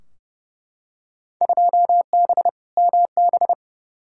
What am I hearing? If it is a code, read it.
Morse code, heard as a keyed tone, about 29 wpm, 700 Hz. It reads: 2B MB
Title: Field Day Exchanges Encore - 30wpm